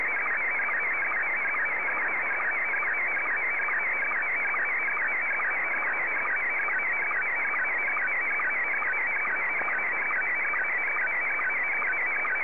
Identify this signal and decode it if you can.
SSTV (slow-scan television), Scottie 1.